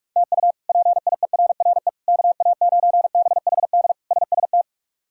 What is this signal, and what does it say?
Morse code, heard as a keyed tone, about 46 wpm, 700 Hz. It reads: TU PIERRE KA9BHD SST
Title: K1USN SST QSO Practice - 45wpm